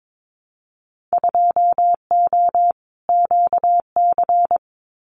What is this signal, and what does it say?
Morse code, heard as a keyed tone, about 22 wpm, 700 Hz. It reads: 2O QC
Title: Winter Field Day Exchanges 22wpm